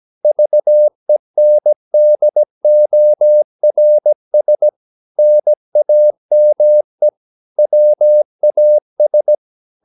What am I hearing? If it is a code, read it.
Morse code, heard as a keyed tone, about 17 wpm, 600 Hz. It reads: VENDORS NAME WAS
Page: Romanen Never af Ken Follet på engelsk.